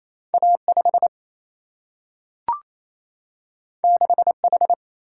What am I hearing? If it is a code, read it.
Morse code, heard as a keyed tone, about 28 wpm, 700 Hz. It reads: A5 6H
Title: Single-Letter Single-Number 28wpm